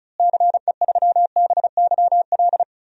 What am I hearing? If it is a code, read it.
Morse code, heard as a keyed tone, about 34 wpm, 700 Hz. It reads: CE3BYL
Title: DXCC-100 Call Signs - 6 Characters - Repeated 3x - 35wpm